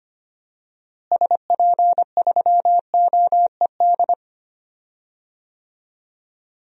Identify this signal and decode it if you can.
Morse code, heard as a keyed tone, about 25 wpm, 700 Hz. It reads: SP3OED
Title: DXCC-100 Call Signs - 6 Characters - Repeated 3x - 25wpm